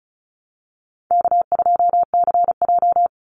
Morse code, heard as a keyed tone, about 34 wpm, 700 Hz. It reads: K2CJ
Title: Easy Call Signs Repeated 2x - 35wpm